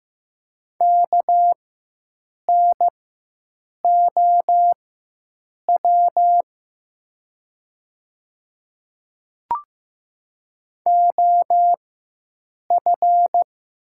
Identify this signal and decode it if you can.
Morse code, heard as a keyed tone, about 15 wpm, spaced as for 7 wpm, 700 Hz. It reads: KNOW OF
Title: 4x Character Spacing - Top 200 Words - 15wpm